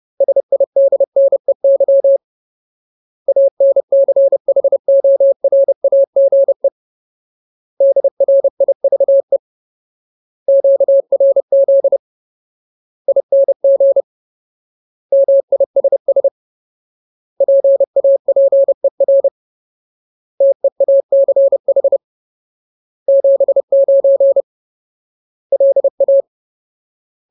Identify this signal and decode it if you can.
Morse code, heard as a keyed tone, about 30 wpm, 550 Hz. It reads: SIDNEY ANCHORAGE DRIVE QRZ ING MISS PAPER TEACH 79 LA
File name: Ultimate Word List / 30WPM Ultimate